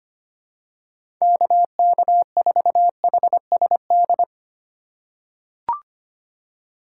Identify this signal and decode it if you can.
Morse code, heard as a keyed tone, about 25 wpm, 700 Hz. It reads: KK4HSD
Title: USA Call Signs - 6 Characters - 4x Character Spacing - 25wpm